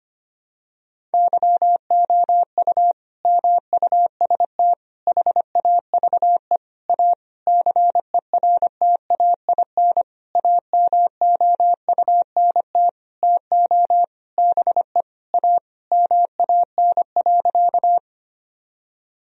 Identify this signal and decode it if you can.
Morse code, heard as a keyed tone, about 25 wpm, 700 Hz. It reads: YOU MUST HAVE A CERTAIN AMOUNT TO BE A MAN.
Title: Sentences from Top 1000 Words - Review Part 4 - 25wpm